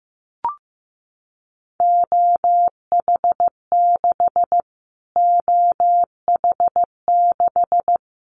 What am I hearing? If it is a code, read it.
Morse code, heard as a keyed tone, about 15 wpm, 700 Hz. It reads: OH6 OH6
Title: Callsign Prefixes for Top 25 DXCC - Repeated 3x - 15wpm